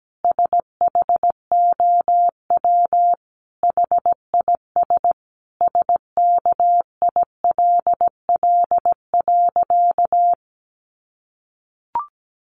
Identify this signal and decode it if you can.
Morse code, heard as a keyed tone, about 17 wpm, 700 Hz. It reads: SHOW HIS SKILL.
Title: Sentences from Top 1200 Words 17wpm